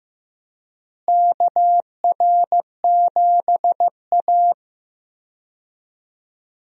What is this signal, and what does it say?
Morse code, heard as a keyed tone, about 15 wpm, 700 Hz. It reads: KR7A
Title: USA Call Signs - 4 Characters - Repeated 2x - 15wpm